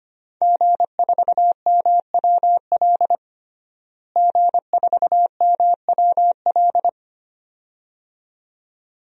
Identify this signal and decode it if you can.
Morse code, heard as a keyed tone, about 25 wpm, 700 Hz. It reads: G4MWL G4MWL
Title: DXCC-100 Call Signs - 5 Characters - Repeated 2x - 25wpm